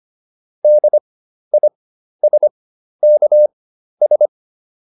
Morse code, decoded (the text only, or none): DISKS